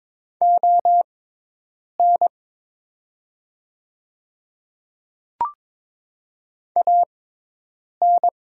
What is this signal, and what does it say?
Morse code, heard as a keyed tone, about 22 wpm, spaced as for 8 wpm, 700 Hz. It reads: ON AN
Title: 6x Character Spacing - Top 100 Words - 22wpm